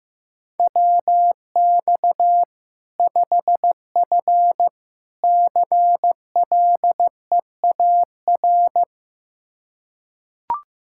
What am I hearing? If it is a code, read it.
Morse code, heard as a keyed tone, about 15 wpm, 700 Hz. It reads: WX 5F CLEAR
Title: Practice QSO - Exchange 2 - Info - 15wpm